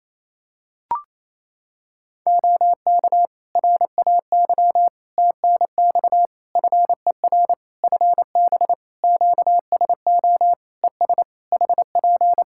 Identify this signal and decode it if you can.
Morse code, heard as a keyed tone, about 28 wpm, 700 Hz. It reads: OK RAY TNX FER FB QSO ES HP
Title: Practice QSO - Exchange 4 - Ending - 28wpm